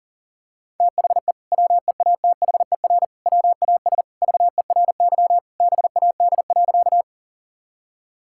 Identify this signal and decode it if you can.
Morse code, heard as a keyed tone, about 40 wpm, 700 Hz. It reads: THE WEATHER WAS VERY BAD.
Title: Sentences from Top 1000 Words - Review Part 5 - 40wpm